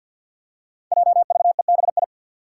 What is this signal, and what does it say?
Morse code, heard as a keyed tone, about 45 wpm, 700 Hz. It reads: JVEBI
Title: Five Challenging Letter-Number Combinations 50wpm